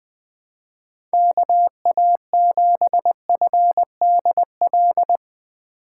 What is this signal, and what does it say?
Morse code, heard as a keyed tone, about 22 wpm, 700 Hz. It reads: KA7FDL
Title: Speed Racing - USA Call Signs 20wpm